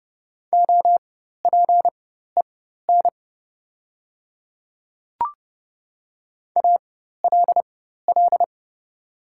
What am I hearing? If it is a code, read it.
Morse code, heard as a keyed tone, about 30 wpm, spaced as for 14 wpm, 700 Hz. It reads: OPEN ALL
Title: 4x Character Spacing - Top 300 Words - 30wpm